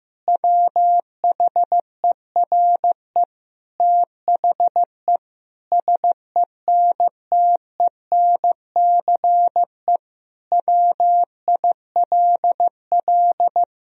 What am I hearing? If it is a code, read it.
Morse code, heard as a keyed tone, about 15 wpm, 700 Hz. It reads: WHERE THE SENTENCE WILL
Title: Sentences from Top 1000 Words - Review Part 1 - 15wpm